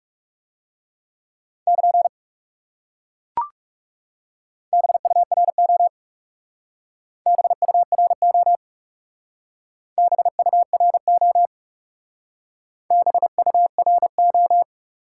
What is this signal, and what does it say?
Morse code, heard as a keyed tone, about 38 wpm, 700 Hz. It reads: <KN> BURO BURO BURO BURO